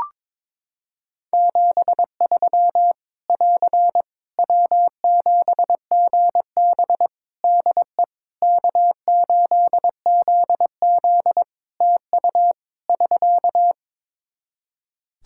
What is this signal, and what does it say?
Morse code, heard as a keyed tone, about 22 wpm, 700 Hz. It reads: 73 <AR> W7GB DE K8ZZ TU <SK>
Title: Practice QSO - Exchange 4 - Ending - 22wpm